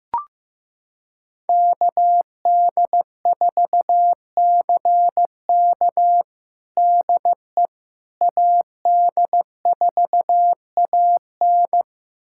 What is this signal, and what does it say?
Morse code, heard as a keyed tone, about 15 wpm, 700 Hz. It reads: KD4CK DE AD4AN
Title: USA Callsign DE Callsign - 5 Characters - 15wpm